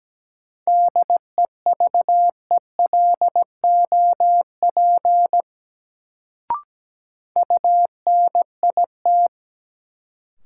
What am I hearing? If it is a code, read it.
Morse code, heard as a keyed tone, about 17 wpm, 700 Hz. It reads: DEVELOP UNIT
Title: Sentences from Top 1000 Words - Review Part 2 - New Word Drill - 17wpm